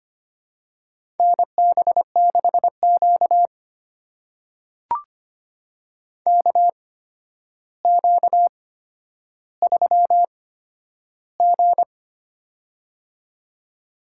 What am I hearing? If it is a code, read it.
Morse code, heard as a keyed tone, about 25 wpm, spaced as for 11 wpm, 700 Hz. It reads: NB6Q KQ3G
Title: USA Call Signs - 4 Characters - 8x Character Spacing - 25wpm